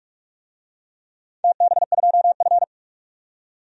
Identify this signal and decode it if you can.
Morse code, heard as a keyed tone, about 44 wpm, 700 Hz. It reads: TB2F
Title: DXCC-100 Call Signs - 4 Characters - 45wpm